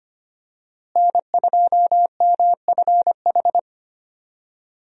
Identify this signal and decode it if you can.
Morse code, heard as a keyed tone, about 25 wpm, 700 Hz. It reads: N2MFH
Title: USA Call Signs - 5 Characters - Repeated 2x - 25wpm